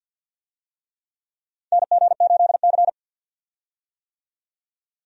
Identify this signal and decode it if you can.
Morse code, heard as a keyed tone, about 53 wpm, 700 Hz. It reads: NG8C
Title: USA Call Signs - 4 Characters - 50wpm